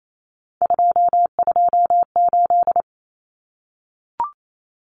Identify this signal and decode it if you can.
Morse code, heard as a keyed tone, about 28 wpm, 700 Hz. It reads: 228
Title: Numbers - 3-Digit 28wpm